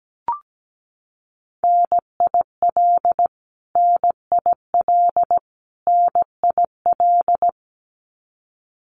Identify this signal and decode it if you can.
Morse code, heard as a keyed tone, about 17 wpm, 700 Hz. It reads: NIL NIL NIL
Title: Abbreviations Repeated 3x - 17wpm